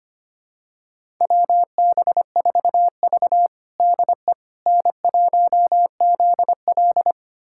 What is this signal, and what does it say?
Morse code, heard as a keyed tone, about 25 wpm, 700 Hz. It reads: WB4V DE N1ZL
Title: USA Callsign DE Callsign - 4 Characters - 25wpm